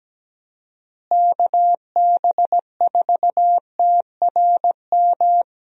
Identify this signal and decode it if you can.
Morse code, heard as a keyed tone, about 17 wpm, 700 Hz. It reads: KB4TRM
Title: USA Call Signs - 6 Characters - Repeated 2x - 17wpm